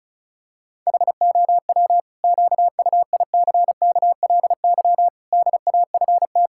Morse code, heard as a keyed tone, about 36 wpm, 700 Hz. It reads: HOW QUICKLY DAFT
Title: Pangram Challenge v2 - 35wpm